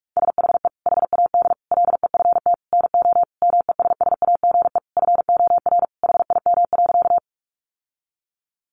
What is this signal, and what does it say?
Morse code, heard as a keyed tone, about 44 wpm, 700 Hz. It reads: SHE HAD LEFT NO MESSAGE FOR HIM.
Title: Sentences from Top 1100 Words 45wpm